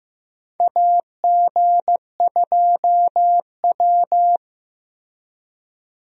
Morse code, heard as a keyed tone, about 15 wpm, 700 Hz. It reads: AG2W
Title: USA Call Signs - 4 Characters - Repeated 3x - 15wpm